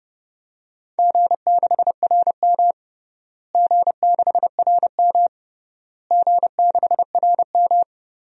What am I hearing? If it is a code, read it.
Morse code, heard as a keyed tone, about 31 wpm, 700 Hz. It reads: G6RM G6RM G6RM
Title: DXCC-100 Call Signs - 4 Characters - Repeated 3x - 30wpm